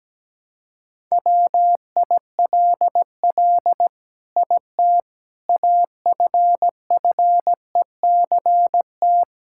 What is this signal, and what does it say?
Morse code, heard as a keyed tone, about 17 wpm, 700 Hz. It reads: WILL IT AFFECT
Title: Sentences from Top 1000 Words - Review Part 6 - 17wpm